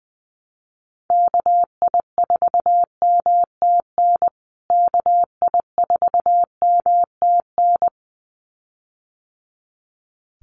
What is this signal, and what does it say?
Morse code, heard as a keyed tone, about 20 wpm, 700 Hz. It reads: KI4MTN KI4MTN
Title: USA Call Signs Repeated 2x 20wpm